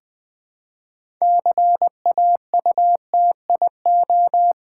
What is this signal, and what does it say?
Morse code, heard as a keyed tone, about 20 wpm, 700 Hz. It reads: CAUTIO
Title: Build A Word - 3 Letters At A Time - 8-Letter Words - 20wpm